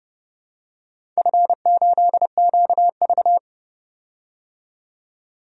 Morse code, heard as a keyed tone, about 30 wpm, 700 Hz. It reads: F8QV
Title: DXCC-100 Call Signs - 4 Characters - 30wpm